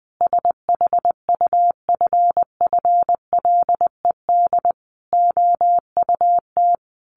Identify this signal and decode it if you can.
Morse code, heard as a keyed tone, about 19 wpm, 700 Hz. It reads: SHUFFLED OUT
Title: A Sound Of Thunder 20wpm